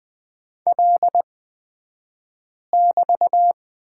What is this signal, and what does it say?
Morse code, heard as a keyed tone, about 20 wpm, spaced as for 5 wpm, 700 Hz. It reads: L<BT>
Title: 249 Lesson 40d - BK - Two Characters 20wpm